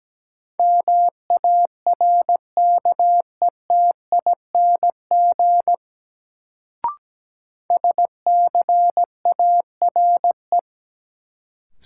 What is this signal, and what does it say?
Morse code, heard as a keyed tone, about 17 wpm, 700 Hz. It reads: MARKETING SCARE
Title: Sentences from Top 1000 Words - Review Part 5 - New Word Drill - 17wpm